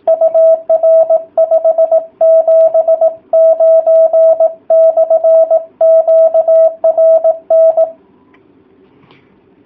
Morse code, heard as a keyed tone, about 18 wpm, 650 Hz. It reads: UR 579/QRN